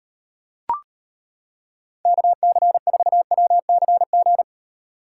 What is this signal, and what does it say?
Morse code, heard as a keyed tone, about 31 wpm, 700 Hz. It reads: KC4WCG
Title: Speed Racing - USA 2x3 Call Signs 25wpm